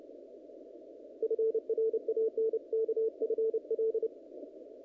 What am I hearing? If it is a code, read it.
Morse code, heard as a keyed tone, about 30 wpm, 450 Hz. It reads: FRANK FL
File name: contests / naqp-cw-k5tr-op-k5pi- -1900